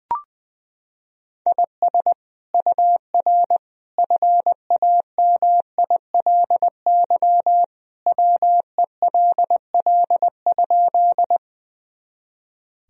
Morse code, heard as a keyed tone, about 20 wpm, 700 Hz. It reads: IS UR FAMILY WELL?
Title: Sentences from Top 400 Words and Abbreviations 20wpm